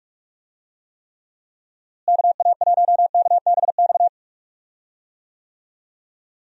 Morse code, heard as a keyed tone, about 46 wpm, 700 Hz. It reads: KA1KBX
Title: USA Call Signs - 6 Characters - Repeated 2x - 45wpm